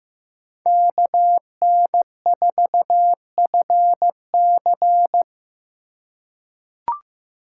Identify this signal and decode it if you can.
Morse code, heard as a keyed tone, about 15 wpm, 700 Hz. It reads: KN4FC
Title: USA Call Signs - 5 Characters - 6x Character Spacing - 15wpm